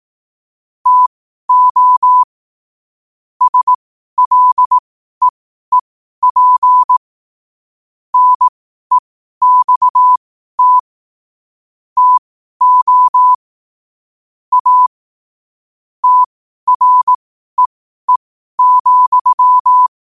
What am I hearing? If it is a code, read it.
Morse code, heard as a keyed tone, about 18 wpm, spaced as for 12 wpm, 1000 Hz. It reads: TO SLEEP NEXT TO A TREE,
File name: qotdfiles / 12wpm